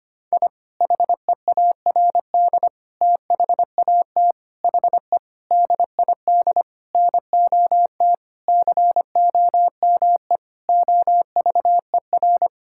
Morse code, heard as a keyed tone, about 25 wpm, 700 Hz. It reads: I HEARD THAT HE DID NOT COME OVER
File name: practice / 25wpm / Sentences with 4-Word Phrases - 25wpm